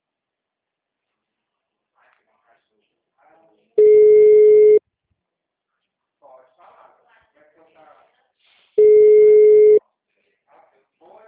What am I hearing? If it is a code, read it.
Ringback tone (Europe, 425 Hz), with speech.
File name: out- -2044- - - .17646